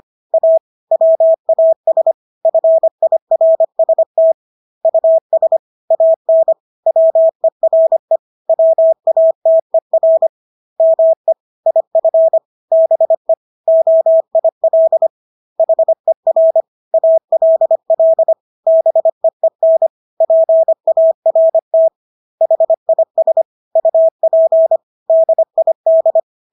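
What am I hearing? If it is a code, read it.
Morse code, heard as a keyed tone, about 25 wpm, 650 Hz. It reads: A WAS FIRST US AN WERE WATER ME IF BE OIL HER ALL BEEN PART HIS UP DID
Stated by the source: Morse practice of English words at 25 wpm from CW med Gnister.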